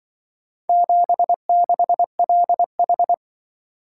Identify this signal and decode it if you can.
Morse code, heard as a keyed tone, about 23 wpm, 700 Hz. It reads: 76LH
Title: Speed-Racing - Four Letter-Number Combinations - 20wpm